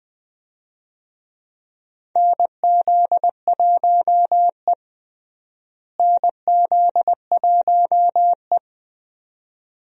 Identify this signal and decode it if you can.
Morse code, heard as a keyed tone, about 20 wpm, 700 Hz. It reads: NZ1E NZ1E
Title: USA Call Signs - 4 Characters - Repeated 2x - 20wpm